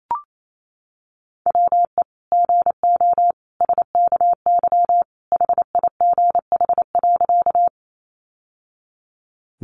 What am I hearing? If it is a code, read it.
Morse code, heard as a keyed tone, about 28 wpm, 700 Hz. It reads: WE GO SKY HIGH.
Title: Sentences from Top 1200 Words 28wpm